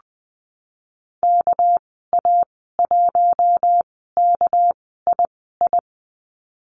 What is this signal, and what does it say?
Morse code, heard as a keyed tone, about 20 wpm, spaced as for 15 wpm, 700 Hz. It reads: KA1KII
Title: 231 Lesson 37g - Letter x - Call Signs 20wpm